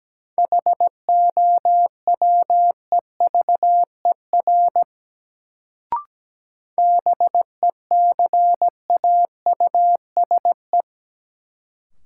Morse code, heard as a keyed tone, about 17 wpm, 700 Hz. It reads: HOWEVER BECAUSE
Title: Sentences from Top 600 Words - New Word Drill - 17wpm